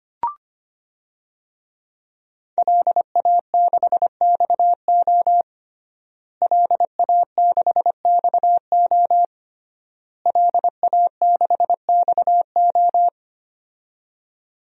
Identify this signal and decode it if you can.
Morse code, heard as a keyed tone, about 25 wpm, 700 Hz. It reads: LA6XO LA6XO LA6XO
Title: DXCC-100 Call Signs - 5 Characters - Repeated 3x - 25wpm